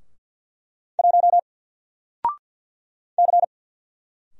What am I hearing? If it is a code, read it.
Morse code, heard as a keyed tone, about 48 wpm, 700 Hz. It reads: 1 C
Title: Single Character (Letter:Number:Punctuation:Prosign) Alt 50wpm